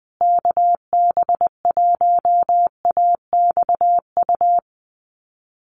Morse code, heard as a keyed tone, about 20 wpm, 700 Hz. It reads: KB1AXU
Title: USA Call Signs Repeated 3x 20wpm